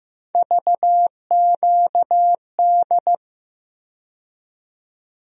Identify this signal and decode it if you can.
Morse code, heard as a keyed tone, about 15 wpm, 700 Hz. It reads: VQD
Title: USA Call Sign Postfixes - 3 Letters 15wpm